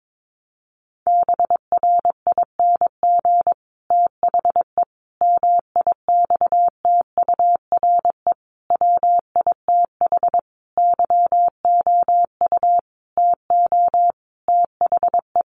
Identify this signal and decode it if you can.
Morse code, heard as a keyed tone, about 22 wpm, 700 Hz. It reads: BRING THE MIXTURE WITH YOU TO THE
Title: Sentences from Top 1500 Words 22wpm